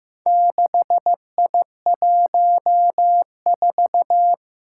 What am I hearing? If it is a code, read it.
Morse code, heard as a keyed tone, about 15 wpm, 700 Hz. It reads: 6I14
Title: Four Challenging Letter-Number Combinations 15wpm